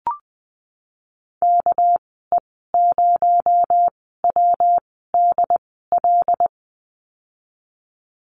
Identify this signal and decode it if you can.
Morse code, heard as a keyed tone, about 19 wpm, spaced as for 14 wpm, 700 Hz. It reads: KE0WDL
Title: 219 Lesson 35g - Number 0 - Call Signs 20wpm